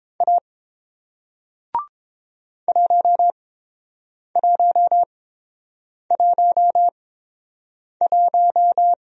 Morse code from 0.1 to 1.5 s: A

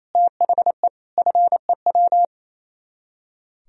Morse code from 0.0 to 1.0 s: THE

